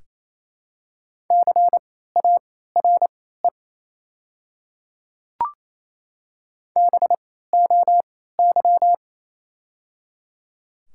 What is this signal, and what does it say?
Morse code, heard as a keyed tone, about 28 wpm, spaced as for 16 wpm, 700 Hz. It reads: CARE BOY
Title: 3x Character Spacing - Top 500 Words - 28wpm